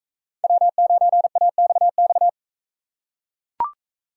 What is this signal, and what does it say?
Morse code, heard as a keyed tone, about 40 wpm, 700 Hz. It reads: W9AXX